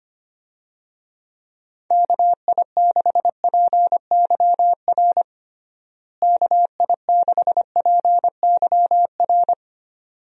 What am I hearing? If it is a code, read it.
Morse code, heard as a keyed tone, about 25 wpm, 700 Hz. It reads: KI6PYR KI6PYR
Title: USA Call Signs - 6 Characters - Repeated 2x - 25wpm